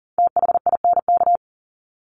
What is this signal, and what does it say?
Morse code, heard as a keyed tone, about 40 wpm, 700 Hz. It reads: THINK
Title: Top 200 Words 40wpm